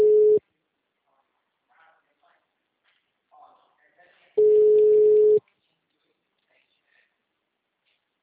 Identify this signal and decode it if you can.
Ringback tone (Europe, 425 Hz), with speech.